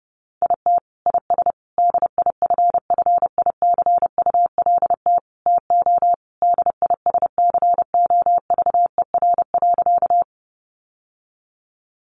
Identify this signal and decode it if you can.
Morse code, heard as a keyed tone, about 30 wpm, 700 Hz. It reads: IT IS DIFFICULT TO DISCOVER.
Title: Sentences from Top 1100 Words 30wpm